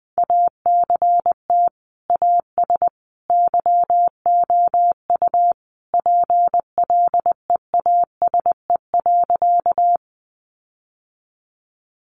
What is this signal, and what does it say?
Morse code, heard as a keyed tone, about 20 wpm, 700 Hz. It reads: ACT AS YOU PLEASE.
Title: Sentences from Top 1500 Words 20wpm